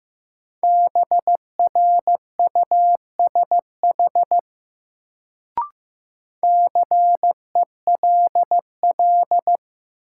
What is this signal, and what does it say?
Morse code, heard as a keyed tone, about 15 wpm, 700 Hz. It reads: BRUSH CELL
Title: Sentences from Top 1100 Words - New Word Drill - 15wpm